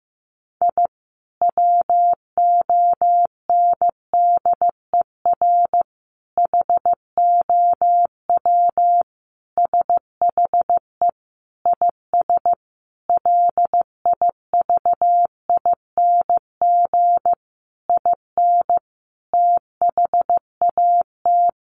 Morse code, heard as a keyed tone, about 15 wpm, 700 Hz. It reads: I WONDER HOW SHE IS LIVING IN THAT
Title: Sentences from Top 1500 Words 15wpm